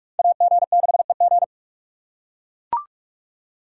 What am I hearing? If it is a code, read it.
Morse code, heard as a keyed tone, about 44 wpm, 700 Hz. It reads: AG6EG